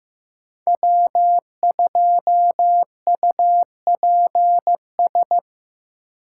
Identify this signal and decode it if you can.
Morse code, heard as a keyed tone, about 15 wpm, 700 Hz. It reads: W2UPS
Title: USA Call Signs - 5 Characters - 6x Character Spacing - 15wpm